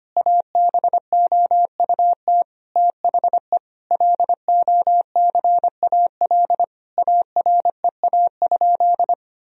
Morse code, heard as a keyed tone, about 25 wpm, 700 Hz. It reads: ABOUT THE LOCAL AREA?
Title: Sentences from Top 1000 Words - Review Part 3 - 25wpm